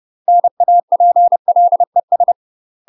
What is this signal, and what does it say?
Morse code, heard as a keyed tone, about 30 wpm, 700 Hz. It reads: NAPLES
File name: UK MASTER QSOs TRAINING/30WPM